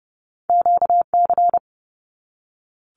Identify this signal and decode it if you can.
Morse code, heard as a keyed tone, about 30 wpm, 700 Hz. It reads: QC
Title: Canadian Provinces and Territories 30wpm